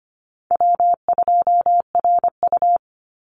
Morse code, heard as a keyed tone, about 25 wpm, 700 Hz. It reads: W2RU
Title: Easy Call Signs - 25wpm